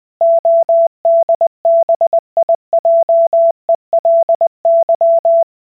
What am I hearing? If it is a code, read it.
Morse code, heard as a keyed tone, about 20 wpm, 670 Hz. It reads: ODBIJELY